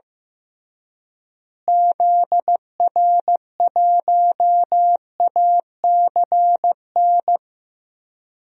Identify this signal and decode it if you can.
Morse code, heard as a keyed tone, about 15 wpm, 700 Hz. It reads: ZR1ACN
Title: DXCC-100 Call Signs - 6 Characters - Repeated 3x - 15wpm